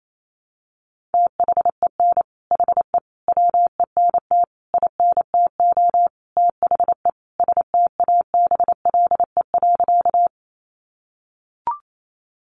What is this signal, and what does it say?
Morse code, heard as a keyed tone, about 28 wpm, 700 Hz. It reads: THEN HE WENT INTO THE STABLE.
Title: Sentences from Top 1500 Words 28wpm